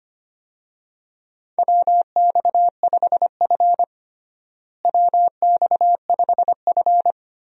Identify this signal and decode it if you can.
Morse code, heard as a keyed tone, about 25 wpm, 700 Hz. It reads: WX5F WX5F
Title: USA Call Signs - 4 Characters - Repeated 3x - 25wpm